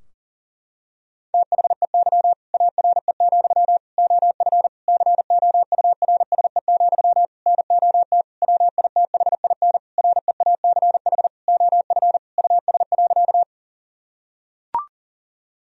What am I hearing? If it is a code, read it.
Morse code, heard as a keyed tone, about 40 wpm, 700 Hz. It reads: THEY ARE, OF COURSE, NOT WITHIN REACH OF US.
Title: Sentences from Top 1000 Words - Review Part 4 - 40wpm